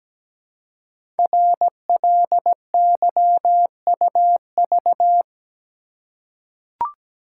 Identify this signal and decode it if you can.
Morse code, heard as a keyed tone, about 17 wpm, 700 Hz. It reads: RLYUV
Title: Code Groups - 5 Letters - 17wpm